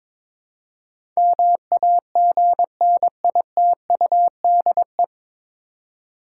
Morse code, heard as a keyed tone, about 22 wpm, 700 Hz. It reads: MAGNITUDE
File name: 9 Letter Words - First 3 Letters Given - 22wpm